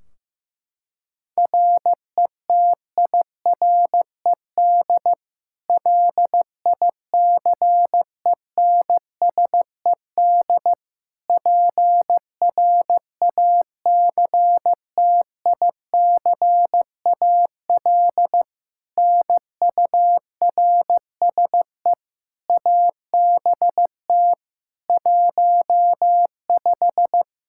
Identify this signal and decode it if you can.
Morse code, heard as a keyed tone, about 15 wpm, 700 Hz. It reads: RETIRED LICENSED PRACTICAL NURSE ABT 15
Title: K1USN SST QSO Practice 15wpm